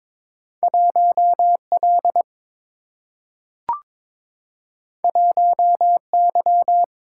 Morse code, heard as a keyed tone, about 22 wpm, 700 Hz. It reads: 1 L 1 Y
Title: Single-Letter Single-Number 22wpm